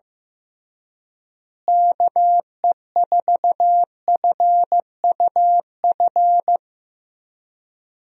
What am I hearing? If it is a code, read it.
Morse code, heard as a keyed tone, about 15 wpm, 700 Hz. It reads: KE4FUF
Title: USA Call Signs - 6 Characters - Repeated 2x - 15wpm